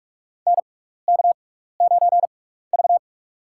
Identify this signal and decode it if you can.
Morse code, heard as a keyed tone, about 44 wpm, spaced as for 15 wpm, 700 Hz. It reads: NK9V
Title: USA Call Signs - 4 Characters - 6x Character Spacing - 45wpm